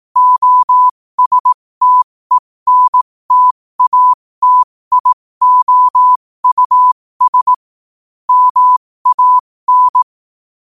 Morse code, heard as a keyed tone, about 18 wpm, spaced as for 15 wpm, 1000 Hz. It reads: OSTENTATIOUS MAN
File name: qotdfiles / 15wpm